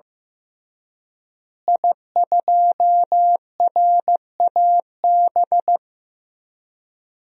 Morse code, heard as a keyed tone, about 15 wpm, 700 Hz. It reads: I2RAB
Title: DXCC-100 Call Signs - 5 Characters - Repeated 3x - 15wpm